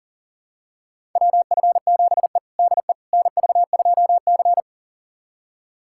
Morse code, heard as a keyed tone, about 37 wpm, 700 Hz. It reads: WF7E DE NV2C
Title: USA Callsign DE Callsign - 4 Characters - 40wpm